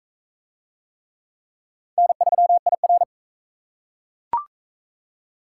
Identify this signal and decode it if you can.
Morse code, heard as a keyed tone, about 44 wpm, 700 Hz. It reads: N3IR